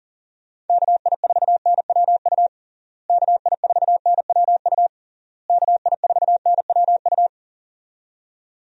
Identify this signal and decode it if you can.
Morse code, heard as a keyed tone, about 37 wpm, 700 Hz. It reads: KI4NWU KI4NWU KI4NWU
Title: USA Call Signs - 6 Characters - Repeated 3x - 40wpm